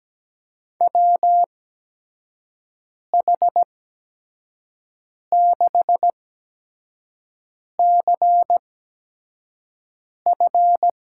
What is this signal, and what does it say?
Morse code, heard as a keyed tone, about 17 wpm, spaced as for 5 wpm, 700 Hz. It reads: WH6CF
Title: USA Call Signs - 5 Characters - 8x Character Spacing - 17wpm